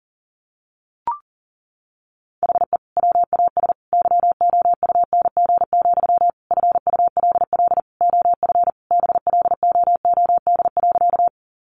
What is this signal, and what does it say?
Morse code, heard as a keyed tone, about 40 wpm, 700 Hz. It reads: HE WAS YOUNG, FULL OF BLOOD.
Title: Sentences from Top 1100 Words 40wpm